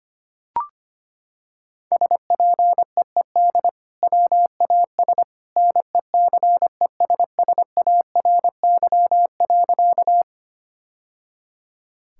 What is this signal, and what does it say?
Morse code, heard as a keyed tone, about 25 wpm, 700 Hz. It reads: SPEED WAS NECESSARY.
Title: Sentences from Top 1000 Words - Review Part 5 - 25wpm